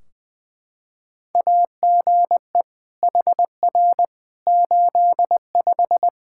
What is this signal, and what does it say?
Morse code, heard as a keyed tone, about 20 wpm, 700 Hz. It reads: AGE HR 85
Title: Practice QSO - Exchange 3 - More Info - 20wpm